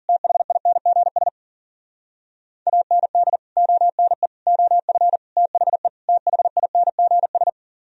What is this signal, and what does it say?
Morse code, heard as a keyed tone, about 52 wpm, 700 Hz. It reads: THINGS AND ONE OF THE THINGS